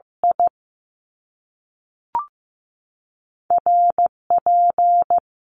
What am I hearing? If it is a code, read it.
Morse code, heard as a keyed tone, about 15 wpm, spaced as for 3 wpm, 700 Hz. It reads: IR P
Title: Top Two Letter Combinations in Words 15wpm